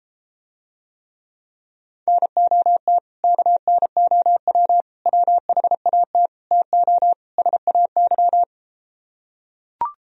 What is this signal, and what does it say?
Morse code, heard as a keyed tone, about 33 wpm, 700 Hz. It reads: NOT KNOW WHAT TO SAY